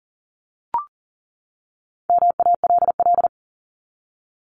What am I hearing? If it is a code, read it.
Morse code, heard as a keyed tone, about 40 wpm, 700 Hz. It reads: MALL
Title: Top 1501-1750 Words 40wpm